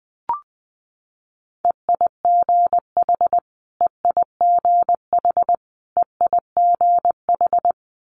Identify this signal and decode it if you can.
Morse code, heard as a keyed tone, about 19 wpm, 700 Hz. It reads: EIGH EIGH EIGH
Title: Multi-letter Phonemes Repeated 3x 20wpm